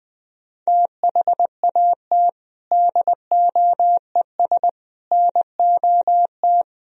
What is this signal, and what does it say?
Morse code, heard as a keyed tone, about 20 wpm, 700 Hz. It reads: THAT DOES NOT
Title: Speed-Racing - Sets of 4 Words 20wpm